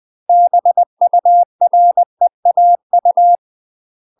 Morse code, heard as a keyed tone, about 20 wpm, 700 Hz. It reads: BUREAU
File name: UK MASTER QSOs TRAINING / 20WPM / UK MASTER QSOs 0032